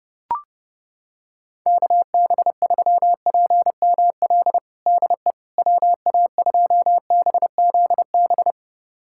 Morse code, heard as a keyed tone, about 30 wpm, 700 Hz. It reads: KB3PML DE WA2BZB
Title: USA Callsign DE Callsign - 6 Characters - 30wpm